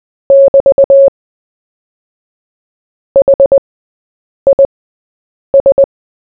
Morse code, heard as a keyed tone, about 20 wpm, spaced as for 8 wpm, 550 Hz. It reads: <BT> HIS
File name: Short-Story-4-8wpm